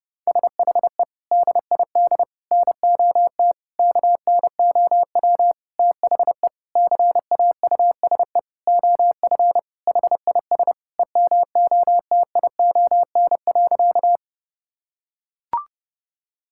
Morse code, heard as a keyed tone, about 30 wpm, 700 Hz. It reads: SHE DID NOT KNOW THE CAUSE OF HIS EMOTION.
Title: Sentences from Top 1600 Words 30wpm